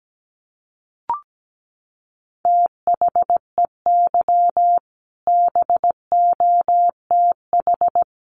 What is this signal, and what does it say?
Morse code, heard as a keyed tone, about 17 wpm, 700 Hz. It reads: THEY BOTH
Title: Sentences from Top 1300 Words 17wpm